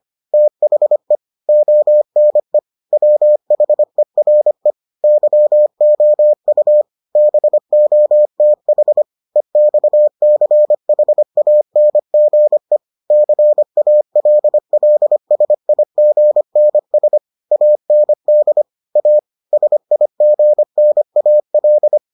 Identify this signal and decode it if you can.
Morse code, heard as a keyed tone, about 25 wpm, 600 Hz. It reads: THE ONE WHERE YOU BOTH EXCHANGE CALLSIGNS AND A SIGNAL